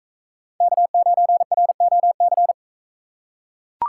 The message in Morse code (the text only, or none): K9ROC